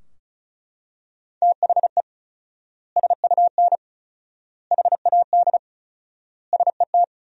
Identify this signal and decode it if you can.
Morse code, heard as a keyed tone, about 36 wpm, 700 Hz. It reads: THE SUN HAD SET
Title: 4x Word-Spacing - Sentences from Top 300 Words - 35wpm